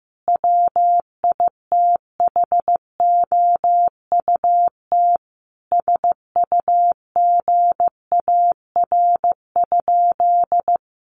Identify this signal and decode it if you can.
Morse code, heard as a keyed tone, about 15 wpm, 700 Hz. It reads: WITHOUT SUGAR?
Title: Sentences from Top 1200 Words 15wpm